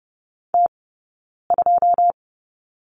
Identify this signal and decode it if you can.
Morse code, heard as a keyed tone, about 30 wpm, spaced as for 9 wpm, 690 Hz. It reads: T2
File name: Lesson 18d - Letter m - Two Characters 30wpm